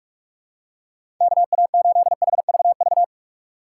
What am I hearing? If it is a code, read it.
Morse code, heard as a keyed tone, about 46 wpm, 700 Hz. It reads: KA8HVV
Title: USA Call Signs - 3x Character Spacing - 45wpm